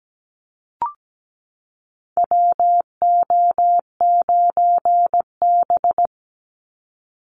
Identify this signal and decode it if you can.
Morse code, heard as a keyed tone, about 17 wpm, 700 Hz. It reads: WO9B
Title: Call Signs Repeated 3x - 17wpm